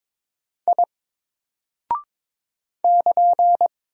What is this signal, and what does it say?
Morse code, heard as a keyed tone, about 22 wpm, 700 Hz. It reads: I <KN>
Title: Single Character (Letter:Number:Punctuation:Prosign) 22wpm